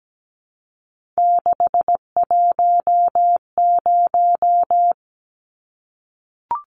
Morse code, heard as a keyed tone, about 17 wpm, 700 Hz. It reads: 610
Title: Numbers - 3-Digit 17wpm